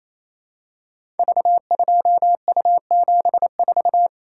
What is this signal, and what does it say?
Morse code, heard as a keyed tone, about 28 wpm, 700 Hz. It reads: V2U74
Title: Code Groups - 5 Letter:Numbers - 28wpm